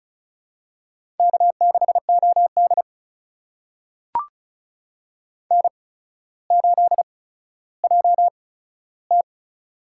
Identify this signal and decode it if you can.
Morse code, heard as a keyed tone, about 36 wpm, spaced as for 15 wpm, 700 Hz. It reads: K6OD N8JT
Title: USA Call Signs - 4 Characters - 8x Character Spacing - 35wpm